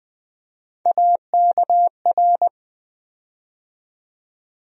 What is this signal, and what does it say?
Morse code, heard as a keyed tone, about 20 wpm, 700 Hz. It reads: AKR
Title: USA Call Sign Postfixes - 3 Letters 20wpm